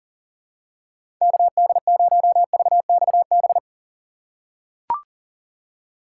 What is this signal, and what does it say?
Morse code, heard as a keyed tone, about 38 wpm, 700 Hz. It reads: KD0VXB
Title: USA Call Signs - 6 Characters - 2x Character Spacing - 40wpm